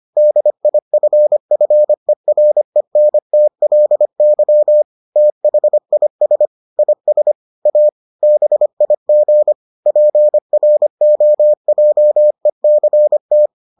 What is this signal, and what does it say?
Morse code, heard as a keyed tone, about 25 wpm, 600 Hz. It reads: DIFFERENTLY THIS IS A BIG PROJECT